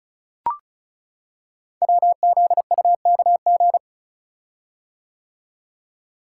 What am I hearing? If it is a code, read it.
Morse code, heard as a keyed tone, about 36 wpm, 690 Hz. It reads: WZUKG
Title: Code Groups - 5 Letters - 35wpm